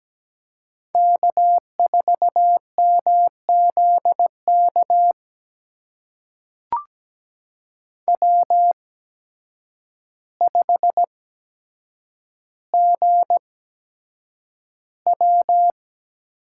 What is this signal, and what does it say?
Morse code, heard as a keyed tone, about 17 wpm, spaced as for 8 wpm, 700 Hz. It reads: K4MZK W5GW
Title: USA Call Signs - 5 Characters - 8x Character Spacing - 17wpm